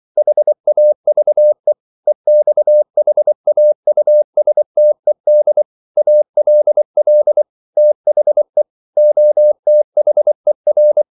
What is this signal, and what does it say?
Morse code, heard as a keyed tone, about 24 wpm, 600 Hz. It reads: HAVE EXHAUSTED ALL THE OTHER
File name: CW / Practice / EnglishQuotations0424